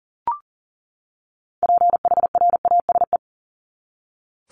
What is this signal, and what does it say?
Morse code, heard as a keyed tone, about 40 wpm, 700 Hz. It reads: PHRASE
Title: Top 1501-1750 Words 40wpm